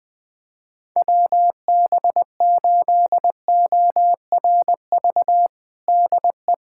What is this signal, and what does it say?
Morse code, heard as a keyed tone, about 20 wpm, 700 Hz. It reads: WB8ORV DE
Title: USA Callsign DE Callsign - 6 Characters - 20wpm